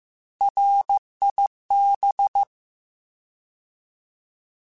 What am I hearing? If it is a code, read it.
Morse code, heard as a keyed tone, about 15 wpm, 800 Hz. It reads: RIB